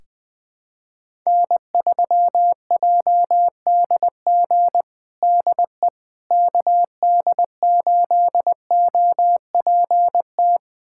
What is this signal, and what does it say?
Morse code, heard as a keyed tone, about 20 wpm, 700 Hz. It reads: N3JDG DE KD8OPT
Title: USA Callsign DE Callsign 20wpm